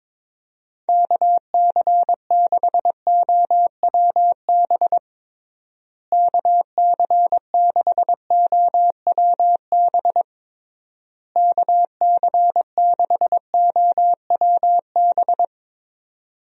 Morse code, heard as a keyed tone, about 22 wpm, 700 Hz. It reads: KC6OWB KC6OWB KC6OWB
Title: USA Call Signs - 6 Characters - Repeated 3x - 22wpm